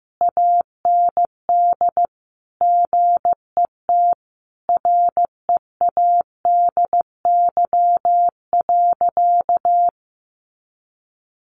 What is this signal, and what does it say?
Morse code, heard as a keyed tone, about 15 wpm, 700 Hz. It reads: AND GET READY.
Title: Sentences from Top 700 Words 15wpm